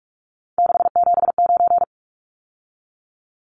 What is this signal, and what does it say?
Morse code, heard as a keyed tone, about 44 wpm, 700 Hz. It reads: 679
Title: Numbers - 3-Digit 45wpm